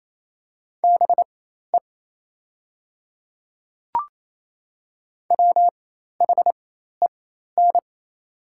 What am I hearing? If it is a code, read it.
Morse code, heard as a keyed tone, about 28 wpm, spaced as for 13 wpm, 700 Hz. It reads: BE WHEN
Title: 4x Character Spacing - Top 200 Words - 28wpm